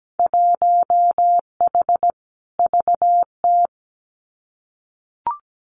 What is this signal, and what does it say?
Morse code, heard as a keyed tone, about 17 wpm, 700 Hz. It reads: 1H VT
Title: Winter Field Day Exchanges Encore 17wpm